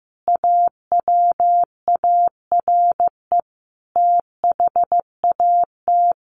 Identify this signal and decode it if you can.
Morse code, heard as a keyed tone, about 15 wpm, 700 Hz. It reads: AWARE THAT
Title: Sentences from Top 1400 Words 15wpm